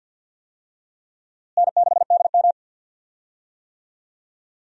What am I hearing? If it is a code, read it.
Morse code, heard as a keyed tone, about 51 wpm, 700 Hz. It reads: N6DM
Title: USA Call Signs - 4 Characters - Repeated 3x - 50wpm